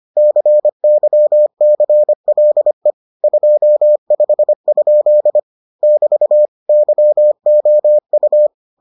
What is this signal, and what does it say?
Morse code, heard as a keyed tone, about 25 wpm, 600 Hz. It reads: CYCLE 25? <BT> YOU